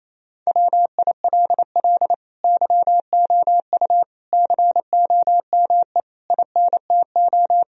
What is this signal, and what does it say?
Morse code, heard as a keyed tone, about 28 wpm, 700 Hz. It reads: WILL YOU COME INTO
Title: Sentences from Top 1000 Words - Review Part 4 - 28wpm